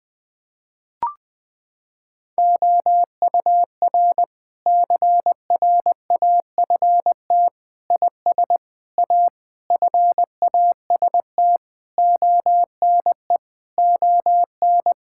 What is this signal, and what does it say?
Morse code, heard as a keyed tone, about 20 wpm, 700 Hz. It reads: OUR CRAFT IS A FAST ONE ON
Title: Sentences from Top 1000 Words - Review Part 6 - 20wpm